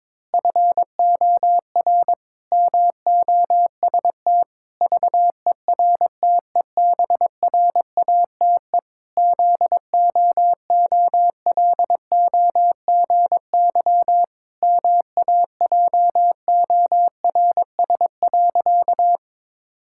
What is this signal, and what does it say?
Morse code, heard as a keyed tone, about 21 wpm, 700 Hz. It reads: FOR MOST VERTEBRATE ZOOLOGY MAJORS.
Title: Pangram Challenge v2 - 22wpm